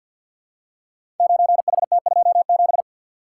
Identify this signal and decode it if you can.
Morse code, heard as a keyed tone, about 45 wpm, 700 Hz. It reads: 9HT27
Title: Code Groups - 5 Letter:Numbers - 50wpm